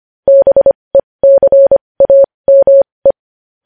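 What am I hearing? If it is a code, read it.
Morse code, heard as a keyed tone, about 25 wpm, spaced as for 20 wpm, 550 Hz. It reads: BECAME
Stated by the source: Morse code practice sending news headlines at 20 wpm.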